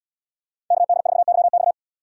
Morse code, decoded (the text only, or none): KD4YX